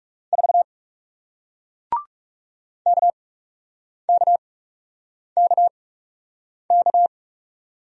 Morse code, heard as a keyed tone, about 38 wpm, spaced as for 8 wpm, 700 Hz. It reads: 4 KKKK